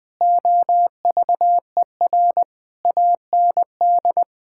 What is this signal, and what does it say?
Morse code, heard as a keyed tone, about 20 wpm, 700 Hz. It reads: OVER AND
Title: Sets of 8 Words - 20wpm